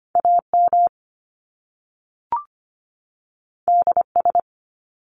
Morse code, heard as a keyed tone, about 25 wpm, 700 Hz. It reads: AM DS
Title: Top Two Letter Combinations in Words 25wpm